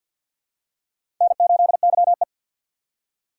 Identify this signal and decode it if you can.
Morse code, heard as a keyed tone, about 46 wpm, 700 Hz. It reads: N8YE
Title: USA Call Signs - 6x Character Spacing - 50wpm